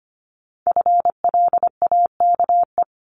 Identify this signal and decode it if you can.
Morse code, heard as a keyed tone, about 25 wpm, 700 Hz. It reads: FLAKE
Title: Multi-letter Phonemes Repeated 3x 25wpm